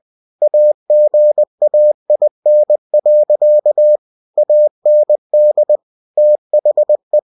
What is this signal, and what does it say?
Morse code, heard as a keyed tone, about 20 wpm, 600 Hz. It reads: AGAIN. AND THE
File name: cw-books / winnie-the-pooh / winnie-20wpm-page-0004